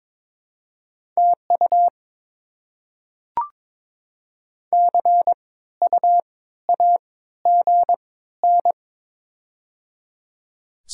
Morse code, heard as a keyed tone, about 22 wpm, spaced as for 14 wpm, 700 Hz. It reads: TU CUAGN
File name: Top 20 QSO Elements - 3x Character Spacing - 22wpm